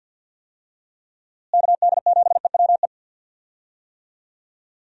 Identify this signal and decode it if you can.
Morse code, heard as a keyed tone, about 52 wpm, 700 Hz. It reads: KD7EWE
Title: USA Call Signs - 6 Characters - Repeated 3x - 50wpm